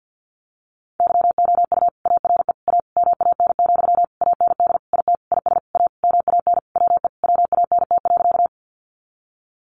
Morse code, heard as a keyed tone, about 50 wpm, 700 Hz. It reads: YOU ARE A MAN, AND IT IS A MAN WE WANT.
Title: Sentences from Top 200 Words 50wpm